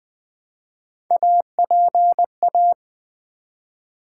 Morse code, heard as a keyed tone, about 20 wpm, 700 Hz. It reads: APA
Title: Build A Word - 1 Letter At A Time - 5-Letter Words - 20wpm